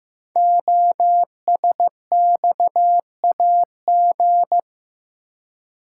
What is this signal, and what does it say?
Morse code, heard as a keyed tone, about 15 wpm, 700 Hz. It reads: OSXAG
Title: Code Groups - 5 Letters - 15wpm